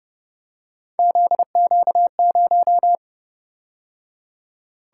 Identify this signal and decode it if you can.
Morse code, heard as a keyed tone, about 29 wpm, 700 Hz. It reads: ZQ0